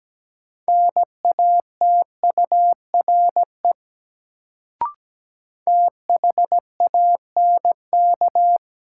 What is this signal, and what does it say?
Morse code, heard as a keyed tone, about 17 wpm, 700 Hz. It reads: NATURE THANK
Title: Sentences from Top 600 Words - New Word Drill - 17wpm